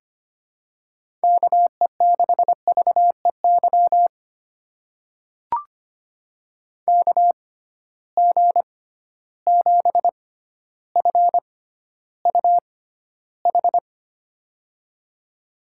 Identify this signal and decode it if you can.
Morse code, heard as a keyed tone, about 25 wpm, spaced as for 13 wpm, 700 Hz. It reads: KE6VEY KG7FUH
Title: USA Call Signs - 6 Characters - 6x Character Spacing - 25wpm